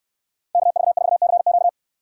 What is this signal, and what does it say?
Morse code, heard as a keyed tone, about 68 wpm, 700 Hz. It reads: KV4CQ